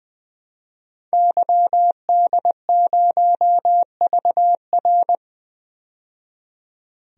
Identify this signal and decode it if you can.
Morse code, heard as a keyed tone, about 20 wpm, 700 Hz. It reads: YD0VR
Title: DXCC-100 Call Signs - 5 Characters - 20wpm